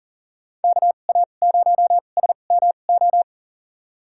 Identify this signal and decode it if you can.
Morse code, heard as a keyed tone, about 40 wpm, spaced as for 29 wpm, 700 Hz. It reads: KA0SMO
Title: USA Call Signs - 6 Characters - 2x Character Spacing - 40wpm